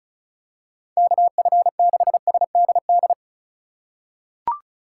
Morse code, heard as a keyed tone, about 36 wpm, 700 Hz. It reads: KF6SDD
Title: USA Call Signs - 6 Characters - 4x Character Spacing - 35wpm